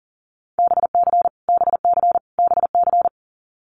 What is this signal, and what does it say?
Morse code, heard as a keyed tone, about 40 wpm, 700 Hz. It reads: BC BC BC
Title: Canadian Provinces and Territories - Repeated 3x - 40wpm